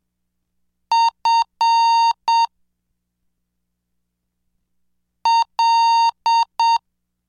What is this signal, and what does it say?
Morse code, heard as a keyed tone, about 7 wpm, 920 Hz. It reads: F L
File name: cursos / CDestudo / 13 Track 13